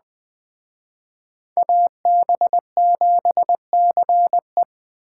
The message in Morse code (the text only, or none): AB7CE